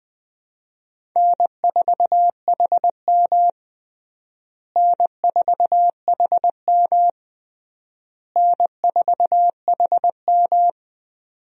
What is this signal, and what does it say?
Morse code, heard as a keyed tone, about 20 wpm, 700 Hz. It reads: N4HM N4HM N4HM
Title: USA Call Signs - 4 Characters - Repeated 3x - 20wpm